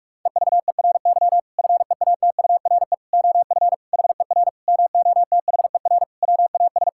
Morse code, heard as a keyed tone, about 46 wpm, 700 Hz. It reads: EVERY FEATURE OF HER MOTHER WAS
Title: Sentences from Top 1000 Words - Review Part 5 - 45wpm